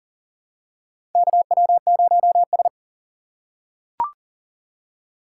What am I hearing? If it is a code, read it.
Morse code, heard as a keyed tone, about 40 wpm, 700 Hz. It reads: KW0S
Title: USA Call Signs - 4 Characters - 2x Character Spacing - 40wpm